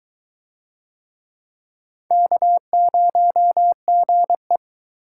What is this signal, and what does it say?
Morse code, heard as a keyed tone, about 22 wpm, 700 Hz. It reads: K0GE